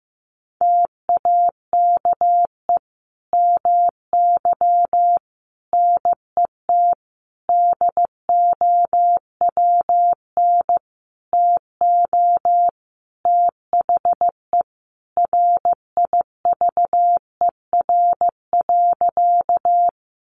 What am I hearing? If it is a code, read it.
Morse code, heard as a keyed tone, about 15 wpm, 700 Hz. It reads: TAKE MY NET DOWN TO THE RIVER.
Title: Sentences from Top 1400 Words 15wpm